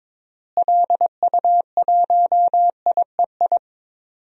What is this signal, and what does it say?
Morse code, heard as a keyed tone, about 21 wpm, 700 Hz. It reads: LU1IEI
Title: DXCC-100 Call Signs - 6 Characters - Repeated 2x - 22wpm